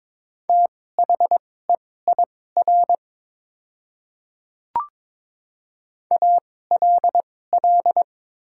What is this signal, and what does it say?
Morse code, heard as a keyed tone, about 22 wpm, spaced as for 16 wpm, 700 Hz. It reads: THEIR ALL
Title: 2x Character Spacing - Top 100 Words - 22wpm